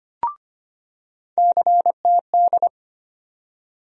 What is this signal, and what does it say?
Morse code, heard as a keyed tone, about 25 wpm, 700 Hz. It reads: CTD
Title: USA Call Sign Postfixes - 3 Letters 25wpm